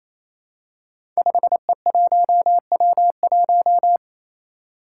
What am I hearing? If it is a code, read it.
Morse code, heard as a keyed tone, about 27 wpm, 700 Hz. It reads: 5E1W1
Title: Five Challenging Letter-Number Combinations 28wpm